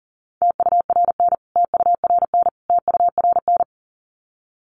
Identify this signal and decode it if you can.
Morse code, heard as a keyed tone, about 40 wpm, 700 Hz. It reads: TURN TURN TURN
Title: Top 500 Words Repeated 40wpm